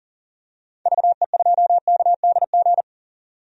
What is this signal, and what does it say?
Morse code, heard as a keyed tone, about 40 wpm, 700 Hz. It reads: VE2KDG
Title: DXCC-100 Call Signs - 6 Characters - 40wpm